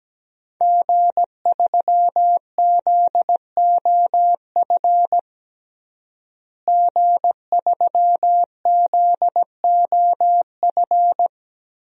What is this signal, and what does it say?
Morse code, heard as a keyed tone, about 17 wpm, 700 Hz. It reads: G3ZOF G3ZOF
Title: DXCC-100 Call Signs - 5 Characters - Repeated 2x - 17wpm